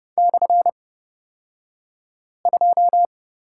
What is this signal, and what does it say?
Morse code, heard as a keyed tone, about 31 wpm, spaced as for 5 wpm, 700 Hz. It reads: /2
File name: Lesson 36d - BT - Two Characters 30wpm